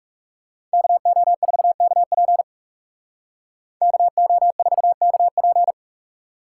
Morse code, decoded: KO4KP KO4KP

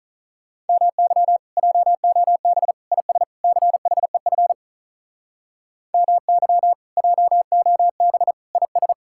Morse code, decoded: MY JOB IS CHEF MY JOB IS

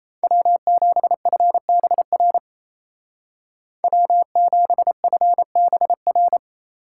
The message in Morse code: W7FBR W7FBR

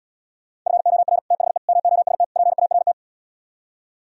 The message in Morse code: FOR THE MOST PART

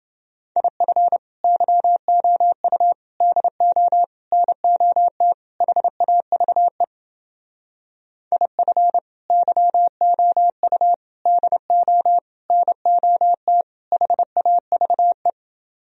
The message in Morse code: IF YOU DO NOT HAVE IF YOU DO NOT HAVE

